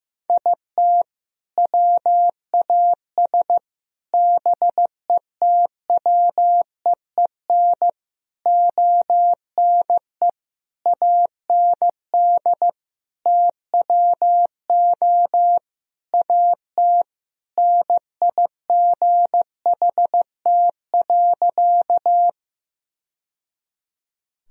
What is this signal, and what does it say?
Morse code, heard as a keyed tone, about 15 wpm, 700 Hz. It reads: IT WAS BETWEEN ONE AND TWO AT NIGHT.
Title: Sentences from Top 1000 Words - Review Part 1 - 15wpm